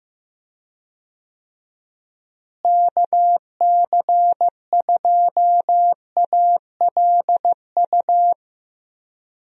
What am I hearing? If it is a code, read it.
Morse code, heard as a keyed tone, about 15 wpm, 700 Hz. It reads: KC2ALU
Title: USA Call Signs - 6 Characters - Repeated 2x - 15wpm